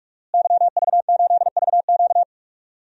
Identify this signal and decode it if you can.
Morse code, heard as a keyed tone, about 46 wpm, 700 Hz. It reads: YV8VQ
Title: DXCC-100 Call Signs - 5 Characters - Repeated 2x - 45wpm